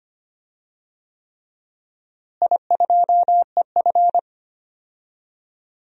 Morse code, heard as a keyed tone, about 25 wpm, 700 Hz. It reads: I2EF
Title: DXCC-100 Call Signs - 4 Characters - Repeated 3x - 25wpm